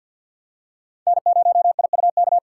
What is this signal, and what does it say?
Morse code, heard as a keyed tone, about 51 wpm, 690 Hz. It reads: N0IUK
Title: USA Call Signs - 5 Characters - 3x Character Spacing - 50wpm